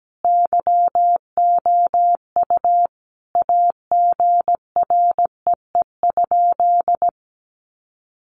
Morse code, heard as a keyed tone, about 17 wpm, 700 Hz. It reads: YOU AGREE?
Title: Sentences from Top 1100 Words 17wpm